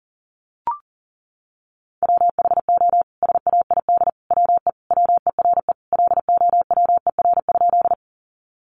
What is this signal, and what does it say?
Morse code, heard as a keyed tone, about 40 wpm, 700 Hz. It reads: WHO SAID WE WERE LOWER?
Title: Sentences from Top 700 Words 40wpm